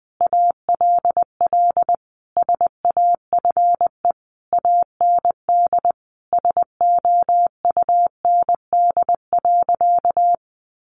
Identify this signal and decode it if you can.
Morse code, heard as a keyed tone, about 20 wpm, 700 Hz. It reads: ALL SAFE AND SOUND.
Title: Sentences from Top 900 Words 20wpm